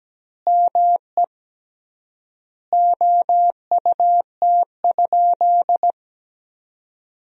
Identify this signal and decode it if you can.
Morse code, heard as a keyed tone, about 17 wpm, 700 Hz. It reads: ME OUT?
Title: 3x Word-Spacing - Sentences from Top 200 Words - 17wpm